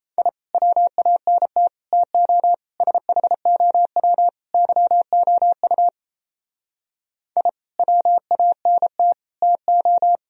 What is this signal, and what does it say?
Morse code, heard as a keyed tone, about 34 wpm, 700 Hz. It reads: I WANT TO SHOW YOU I WANT TO
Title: Speed-Racing - Sets of 5 Words 28wpm